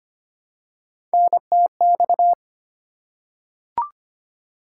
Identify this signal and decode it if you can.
Morse code, heard as a keyed tone, about 25 wpm, 700 Hz. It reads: NTX
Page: morse-code-ninja-practice-25wpm